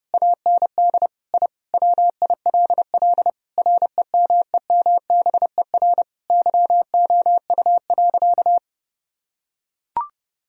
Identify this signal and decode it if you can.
Morse code, heard as a keyed tone, about 30 wpm, 700 Hz. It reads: AND I WILL REMEMBER YOU.
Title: Sentences from Top 1000 Words - Review Part 2 - 30wpm